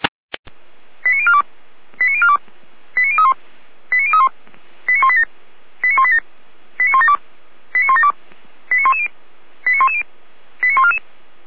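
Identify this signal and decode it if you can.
Five-tone selective call (ZVEI), heard as quick runs of five tones, tones of about 70 ms. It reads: 89042 89042 89031 89031 89187 89187 89172 89172 89110 89110 89144
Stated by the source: Five-tone ZVEI selective call received on SDR.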